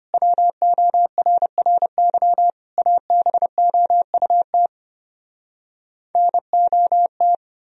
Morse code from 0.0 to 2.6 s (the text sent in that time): WORRY